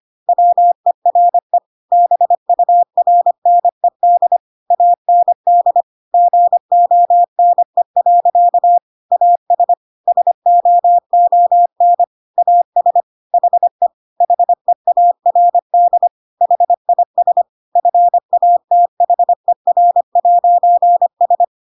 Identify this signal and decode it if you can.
Morse code, heard as a keyed tone, about 25 wpm, 700 Hz. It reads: WERE BURNED AND GONE. AS SOON AS HE HEARD HIS FATHER'S
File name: Pinocchio / Pinocchio ch7 25 25 ik1hge